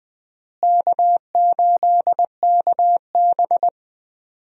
Morse code, heard as a keyed tone, about 21 wpm, 700 Hz. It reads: K8KB
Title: USA Call Signs - 4 Characters - Repeated 3x - 20wpm